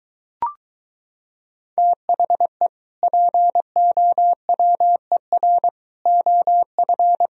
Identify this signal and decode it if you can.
Morse code, heard as a keyed tone, about 23 wpm, 700 Hz. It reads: THE POWER OF